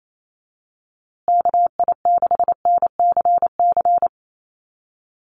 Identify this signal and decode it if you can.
Morse code, heard as a keyed tone, about 28 wpm, 700 Hz. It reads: KI6NCC
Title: USA Call Signs Repeated 3x 28wpm